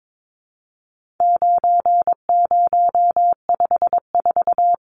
Morse code, heard as a keyed tone, about 22 wpm, 700 Hz. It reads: 9054
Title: Numbers - 4-Digit 22wpm